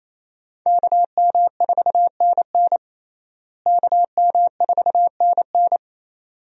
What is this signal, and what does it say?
Morse code, heard as a keyed tone, about 28 wpm, 700 Hz. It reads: KM4NN KM4NN
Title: USA Call Signs - 5 Characters - Repeated 2x - 28wpm